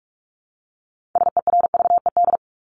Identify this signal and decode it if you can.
Morse code, heard as a keyed tone, about 44 wpm, 700 Hz. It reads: SERVED
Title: ED Suffix - Top 100 Words with Base Words 45wpm